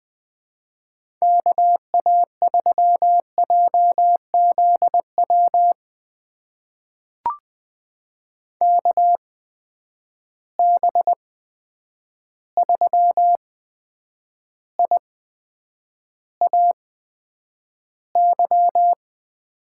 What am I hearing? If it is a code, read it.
Morse code, heard as a keyed tone, about 21 wpm, spaced as for 9 wpm, 700 Hz. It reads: KA3JZW KB3IAY
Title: USA Call Signs - 6 Characters - 8x Character Spacing - 20wpm